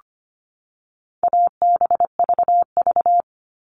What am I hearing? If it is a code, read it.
Morse code, heard as a keyed tone, about 25 wpm, 700 Hz. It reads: ABVV
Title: All Cut Numbers - 4-Digit 25wpm